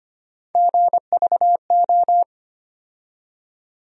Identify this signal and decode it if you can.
Morse code, heard as a keyed tone, about 25 wpm, 700 Hz. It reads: GVO
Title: USA Call Sign Postfixes - 3 Letters 25wpm